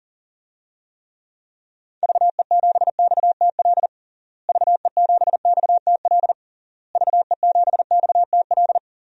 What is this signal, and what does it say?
Morse code, heard as a keyed tone, about 40 wpm, 700 Hz. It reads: VE7XTL VE7XTL VE7XTL
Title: DXCC-100 Call Signs - 6 Characters - Repeated 3x - 40wpm